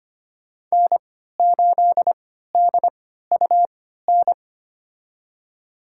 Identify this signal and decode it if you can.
Morse code, heard as a keyed tone, about 25 wpm, spaced as for 14 wpm, 700 Hz. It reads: N8DUN
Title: USA Call Signs - 5 Characters - 3x Character Spacing - 25wpm